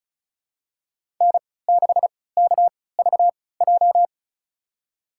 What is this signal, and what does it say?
Morse code, heard as a keyed tone, about 36 wpm, spaced as for 20 wpm, 700 Hz. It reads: N6KVJ
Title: USA Call Signs - 5 Characters - 3x Character Spacing - 35wpm